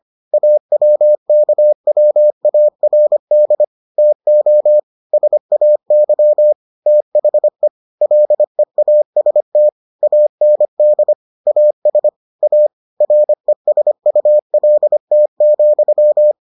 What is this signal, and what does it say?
Morse code, heard as a keyed tone, about 25 wpm, 600 Hz. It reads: AWKWARD TO SAY THE LEAST AND AS A RESULT,